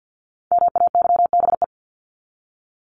Morse code, heard as a keyed tone, about 50 wpm, 700 Hz. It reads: MAYBE
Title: Top 751-1000 Words 50wpm